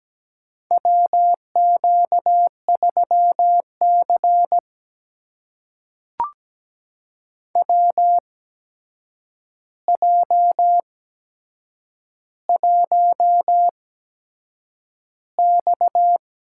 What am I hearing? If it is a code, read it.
Morse code, heard as a keyed tone, about 17 wpm, spaced as for 7 wpm, 700 Hz. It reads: WQ3C WJ1X
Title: USA Call Signs - 4 Characters - 8x Character Spacing - 17wpm